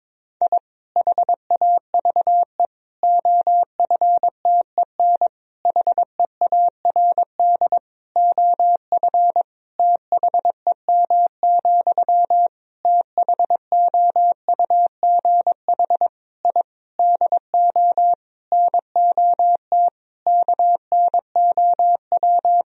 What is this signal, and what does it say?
Morse code, heard as a keyed tone, about 22 wpm, 700 Hz. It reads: I HAVE OFTEN HEARD OF THEM, THOUGH I DO NOT KNOW
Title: Sentences from Top 1000 Words - Review Part 2 - 22wpm